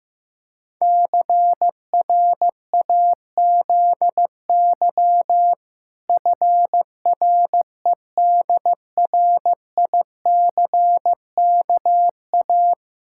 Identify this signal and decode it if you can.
Morse code, heard as a keyed tone, about 15 wpm, 700 Hz. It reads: CRAZY FREDRICKA
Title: Pangram Challenge v2 - 15wpm